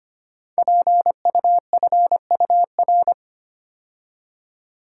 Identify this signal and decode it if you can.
Morse code, heard as a keyed tone, about 25 wpm, 700 Hz. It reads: PUFUR
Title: Code Groups - 5 Letters - 25wpm